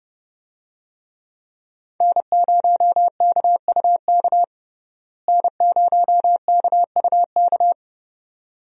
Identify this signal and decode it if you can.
Morse code, heard as a keyed tone, about 30 wpm, 700 Hz. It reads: N0KUK N0KUK
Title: USA Call Signs - 5 Characters - Repeated 2x - 30wpm